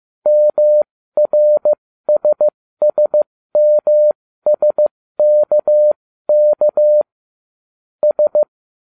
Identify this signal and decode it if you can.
Morse code, heard as a keyed tone, about 15 wpm, spaced as for 13 wpm, 600 Hz. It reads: MRSSMSKK S